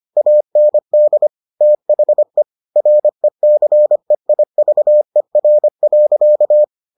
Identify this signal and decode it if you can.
Morse code, heard as a keyed tone, about 25 wpm, 600 Hz. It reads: AND THE RECEIVER.